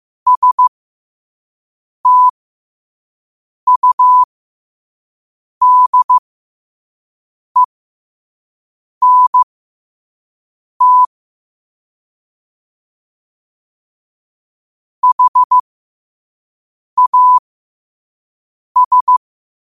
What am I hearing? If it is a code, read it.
Morse code, heard as a keyed tone, about 15 wpm, spaced as for 5 wpm, 1000 Hz. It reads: STUDENT HAS
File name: qotdfiles / 5wpm